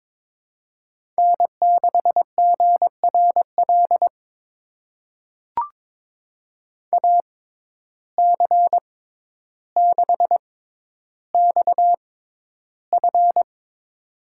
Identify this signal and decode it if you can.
Morse code, heard as a keyed tone, about 23 wpm, spaced as for 11 wpm, 700 Hz. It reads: N6GRL AC6XF
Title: USA Call Signs - 5 Characters - 6x Character Spacing - 22wpm